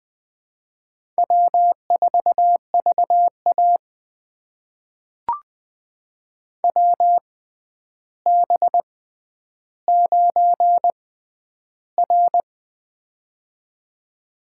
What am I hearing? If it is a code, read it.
Morse code, heard as a keyed tone, about 20 wpm, spaced as for 10 wpm, 700 Hz. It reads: W4VA WB9R
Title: USA Call Signs - 4 Characters - 6x Character Spacing - 20wpm